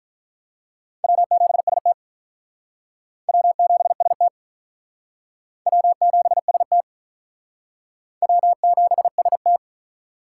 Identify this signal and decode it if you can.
Morse code, heard as a keyed tone, about 42 wpm, 700 Hz. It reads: W7ST W7ST W7ST W7ST
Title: Speed Racing - USA Call Signs 35wpm